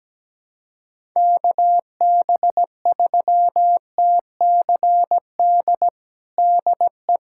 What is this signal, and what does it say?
Morse code, heard as a keyed tone, about 17 wpm, 700 Hz. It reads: KB3TCD DE
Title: USA Callsign DE Callsign - 6 Characters - 17wpm